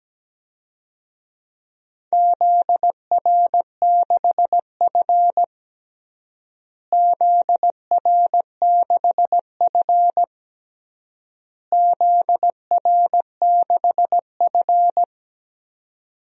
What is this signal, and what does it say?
Morse code, heard as a keyed tone, about 17 wpm, 700 Hz. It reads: ZR6F ZR6F ZR6F
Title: DXCC-100 Call Signs - 4 Characters - Repeated 3x - 17wpm